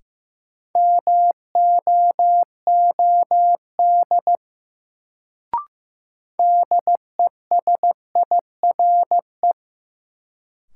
Morse code, heard as a keyed tone, about 15 wpm, 700 Hz. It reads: MOOD DESIRE
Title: Sentences from Top 1100 Words - New Word Drill - 15wpm